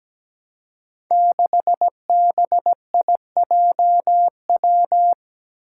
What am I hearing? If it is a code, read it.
Morse code, heard as a keyed tone, about 17 wpm, 700 Hz. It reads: 6BIJW
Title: Five Challenging Letter-Number Combinations 17wpm